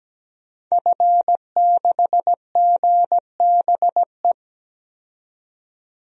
Morse code, heard as a keyed tone, about 17 wpm, 700 Hz. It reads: F6GBE
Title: DXCC-100 Call Signs - 5 Characters - 17wpm